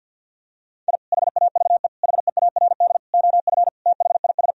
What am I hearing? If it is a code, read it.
Morse code, heard as a keyed tone, about 51 wpm, 700 Hz. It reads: I HAVE HEARD OF THIS
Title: Sentences from Top 1000 Words - Review Part 1 - 50wpm